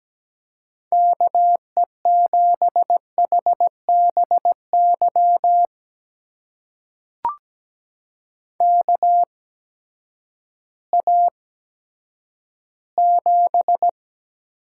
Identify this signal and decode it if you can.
Morse code, heard as a keyed tone, about 17 wpm, spaced as for 10 wpm, 700 Hz. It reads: KE7HBY KA7
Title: USA Call Signs - 6 Characters - 8x Character Spacing - 17wpm